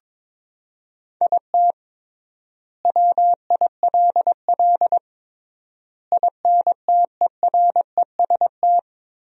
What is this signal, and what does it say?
Morse code, heard as a keyed tone, about 22 wpm, 700 Hz. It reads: IT WILL INTEREST
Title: 3x Word-Spacing - Sentences from Top 400 Words - 22wpm